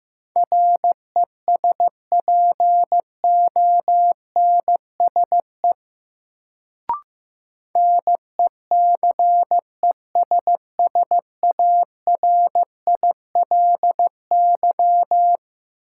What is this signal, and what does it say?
Morse code, heard as a keyed tone, about 15 wpm, 700 Hz. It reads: RESPONSE NECESSARILY
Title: Sentences from Top 1400 Words - New Word Drill Alt - 15wpm